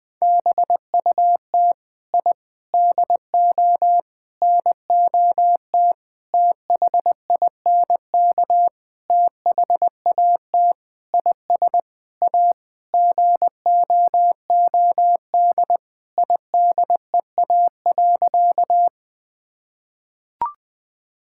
Morse code, heard as a keyed tone, about 20 wpm, 700 Hz. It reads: BUT I DO NOT THINK THAT IS A GOOD IDEA.
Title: Sentences with 5-Word Phrases - Spoken Beginning - 20wpm